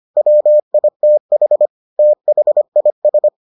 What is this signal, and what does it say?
Morse code, heard as a keyed tone, about 25 wpm, 600 Hz. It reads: WITH THIS